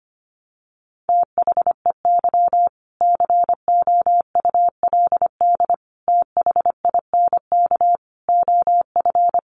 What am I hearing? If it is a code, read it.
Morse code, heard as a keyed tone, about 25 wpm, 700 Hz. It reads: THEY COULD THINK OF
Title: Sentences from Top 800 Words 25wpm